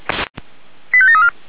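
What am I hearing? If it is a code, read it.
Five-tone selective call (ZVEI), heard as quick runs of five tones, tones of about 70 ms. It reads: 85734
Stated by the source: ZVEI five-tone selective call received by SDR, logged by Feuerwehr Orlamünde fire brigade.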